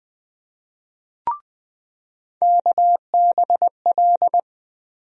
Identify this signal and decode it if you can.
Morse code, heard as a keyed tone, about 20 wpm, 700 Hz. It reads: KBL
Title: USA Call Sign Postfixes - 3 Letters 20wpm